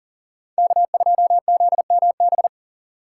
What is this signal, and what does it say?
Morse code, heard as a keyed tone, about 54 wpm, spaced as for 47 wpm, 700 Hz. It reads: K2ZMB
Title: Speed Racing - USA 1x3 Call Signs 40wpm